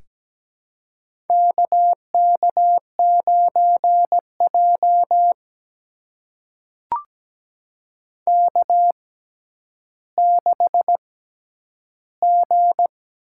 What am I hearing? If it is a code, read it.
Morse code, heard as a keyed tone, about 17 wpm, spaced as for 10 wpm, 700 Hz. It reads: KK9J K6G
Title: USA Call Signs - 4 Characters - 6x Character Spacing - 17wpm